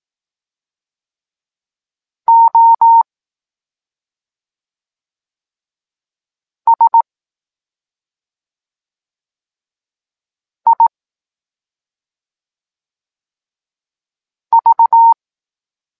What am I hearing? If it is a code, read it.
Morse code, heard as a keyed tone, about 18 wpm, 920 Hz. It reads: O S I V
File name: Tekst-9-rownomierny